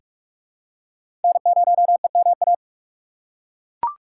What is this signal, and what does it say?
Morse code, heard as a keyed tone, about 42 wpm, 700 Hz. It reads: N0EMA
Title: USA Call Signs - 5 Characters - 3x Character Spacing - 45wpm